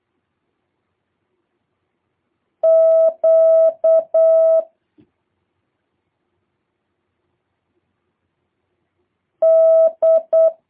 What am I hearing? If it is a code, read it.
Morse code, heard as a keyed tone, about 8 wpm, 650 Hz. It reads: Q D